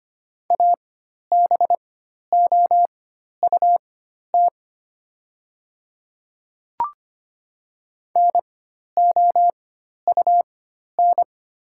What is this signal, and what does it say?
Morse code, heard as a keyed tone, about 25 wpm, spaced as for 12 wpm, 700 Hz. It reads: ABOUT NOUN
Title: 4x Character Spacing - Top 500 Words - 25wpm